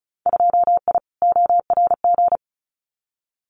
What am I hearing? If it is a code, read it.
Morse code, heard as a keyed tone, about 36 wpm, 700 Hz. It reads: 2I ORG
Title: Winter Field Day Exchanges Encore 35wpm